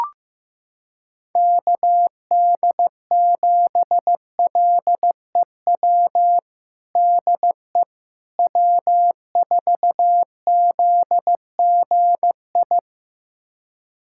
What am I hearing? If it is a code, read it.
Morse code, heard as a keyed tone, about 15 wpm, 700 Hz. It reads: KD7LEW DE W4ZGI
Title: USA Callsign DE Callsign 15wpm